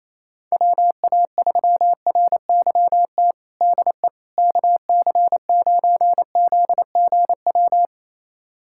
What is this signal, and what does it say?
Morse code, heard as a keyed tone, about 28 wpm, 700 Hz. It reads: WA3RYT DE KC9ZGW
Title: USA Callsign DE Callsign 28wpm